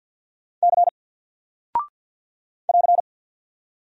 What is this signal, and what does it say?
Morse code, heard as a keyed tone, about 48 wpm, 700 Hz. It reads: C <AR>
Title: Single Character (Letter:Number:Punctuation:Prosign) Alt 50wpm